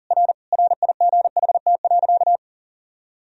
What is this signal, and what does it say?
Morse code, heard as a keyed tone, about 40 wpm, 700 Hz. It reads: R RIGHT.
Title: Sentences from Top 200 Words and Abbreviations 40wpm